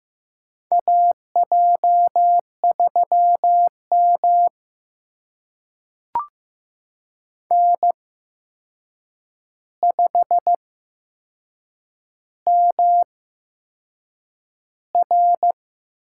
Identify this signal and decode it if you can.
Morse code, heard as a keyed tone, about 15 wpm, spaced as for 6 wpm, 700 Hz. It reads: AJ3M N5MR
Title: USA Call Signs - 8x Character Spacing - 15wpm